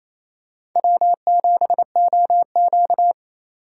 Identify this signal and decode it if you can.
Morse code, heard as a keyed tone, about 28 wpm, 700 Hz. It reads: W7OQ
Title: K1USN SST QSO Practice - 28wpm